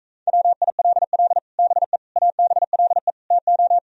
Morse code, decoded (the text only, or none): WILL BE ABLE TO